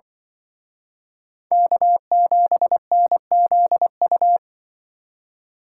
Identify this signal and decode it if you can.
Morse code, heard as a keyed tone, about 25 wpm, 700 Hz. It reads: K7NZU